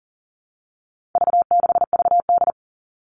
Morse code, heard as a keyed tone, about 40 wpm, 700 Hz. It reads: V6VD
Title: All Cut Numbers - 4-Digit 40wpm